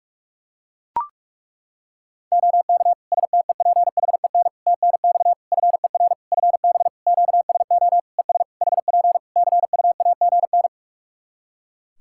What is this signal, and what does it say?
Morse code, heard as a keyed tone, about 41 wpm, 700 Hz. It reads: OK STEPHEN TNX FER FB QSO ES HP CUAGN
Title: Practice QSO - Exchange 4 - Ending - 45wpm